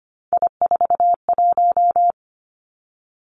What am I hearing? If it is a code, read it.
Morse code, heard as a keyed tone, about 25 wpm, 700 Hz. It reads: I41
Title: Three Challenging Letter-Number Combinations 25wpm